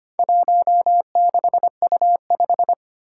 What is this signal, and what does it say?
Morse code, heard as a keyed tone, about 25 wpm, 700 Hz. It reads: 16U5
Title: Four Challenging Letter-Number Combinations 25wpm